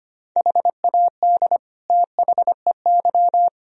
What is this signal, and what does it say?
Morse code, heard as a keyed tone, about 25 wpm, 700 Hz. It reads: HAD THEY
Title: Sentences from Top 1000 Words - Review Part 2 - 25wpm